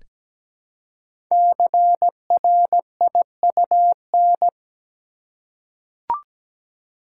Morse code, heard as a keyed tone, about 17 wpm, 700 Hz. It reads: CRIUN
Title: Code Groups - 5 Letters - 17wpm